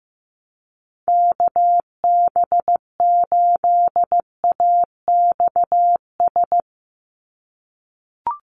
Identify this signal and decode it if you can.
Morse code, heard as a keyed tone, about 15 wpm, 700 Hz. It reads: KB8AXS
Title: USA Call Signs 15wpm